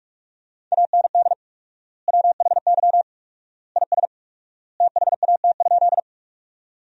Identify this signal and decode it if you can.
Morse code, heard as a keyed tone, about 45 wpm, 700 Hz. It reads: AND WHY IS THAT?
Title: 4x Word-Spacing - Sentences from Top 200 Words - 45wpm